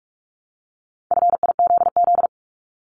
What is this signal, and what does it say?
Morse code, heard as a keyed tone, about 46 wpm, 700 Hz. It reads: FIZZ
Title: Multi-letter Phonemes Repeated 3x 45wpm